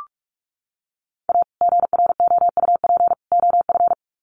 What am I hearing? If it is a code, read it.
Morse code, heard as a keyed tone, about 44 wpm, 700 Hz. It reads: A GROUP OF
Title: Sets of 3 Words 45wpm